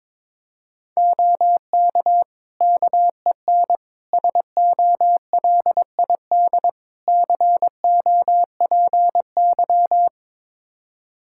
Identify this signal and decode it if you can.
Morse code, heard as a keyed tone, about 22 wpm, 700 Hz. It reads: OK KEN SOLID COPY
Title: Practice QSO - Exchange 3 - More Info - 22wpm